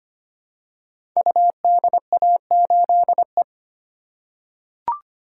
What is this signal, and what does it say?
Morse code, heard as a keyed tone, about 25 wpm, 700 Hz. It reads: UDA8E
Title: Code Groups - 5 Letter:Numbers - 25wpm